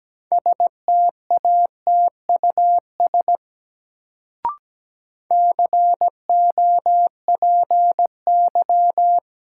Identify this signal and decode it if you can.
Morse code, heard as a keyed tone, about 17 wpm, 700 Hz. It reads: STATUS COPY
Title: Sentences from Top 1400 Words - New Word Drill Alt - 17wpm